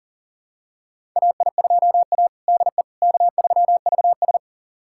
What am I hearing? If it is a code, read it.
Morse code, heard as a keyed tone, about 40 wpm, 700 Hz. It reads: AI2A DE K3VS
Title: USA Callsign DE Callsign - 4 Characters - 40wpm